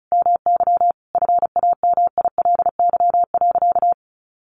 Morse code, heard as a keyed tone, about 34 wpm, 700 Hz. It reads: MY FAMILY.
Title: Sentences from Top 1200 Words 35wpm